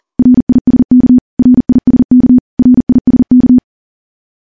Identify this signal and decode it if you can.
Morse code, heard as a keyed tone, about 39 wpm, 250 Hz. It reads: RISK RISK RISK
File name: morse / morse-training-40WPM